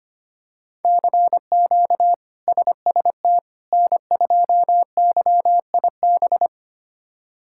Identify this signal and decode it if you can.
Morse code, heard as a keyed tone, about 25 wpm, 700 Hz. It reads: CQ SST N2YIB
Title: K1USN SST QSO Practice - 25wpm